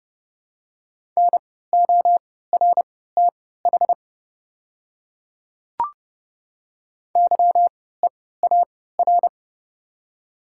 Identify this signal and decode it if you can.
Morse code, heard as a keyed tone, about 32 wpm, spaced as for 17 wpm, 700 Hz. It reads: NORTH YEAR
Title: 3x Character Spacing - Top 300 Words - 30wpm